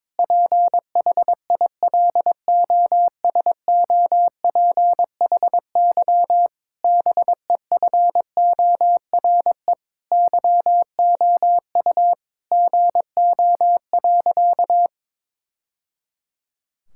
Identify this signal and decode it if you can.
Morse code, heard as a keyed tone, about 22 wpm, 700 Hz. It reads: PHILOSOPHY BEFORE YOU GO.
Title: Sentences from Top 1000 Words - Review Part 6 - 22wpm